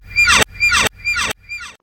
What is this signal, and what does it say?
There is no code.